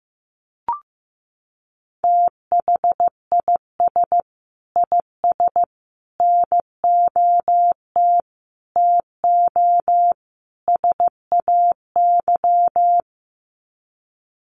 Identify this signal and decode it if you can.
Morse code, heard as a keyed tone, about 15 wpm, 700 Hz. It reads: THIS IS NOT TO SAY
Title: Sets of 5 Words Encore 15wpm